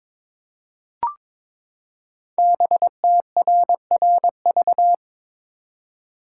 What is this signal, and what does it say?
Morse code, heard as a keyed tone, about 22 wpm, 690 Hz. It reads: BTRRV
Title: Code Groups - 5 Letters - 22wpm